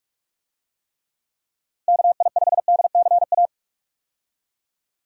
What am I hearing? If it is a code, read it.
Morse code, heard as a keyed tone, about 46 wpm, 700 Hz. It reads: KI5DCA
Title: USA Call Signs - 6 Characters - 45wpm